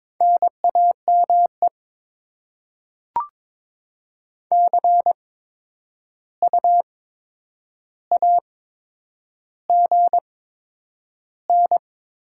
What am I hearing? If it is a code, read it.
Morse code, heard as a keyed tone, about 22 wpm, spaced as for 9 wpm, 700 Hz. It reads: NAME CU AGN
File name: Top 20 QSO Elements - 8x Character Spacing - 22wpm